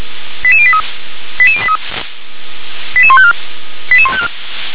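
Five-tone selective call (ZVEI), heard as quick runs of five tones, tones of about 70 ms. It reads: 88083 88083 88154 88154
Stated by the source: Five-tone ZVEI selective call received on SDR.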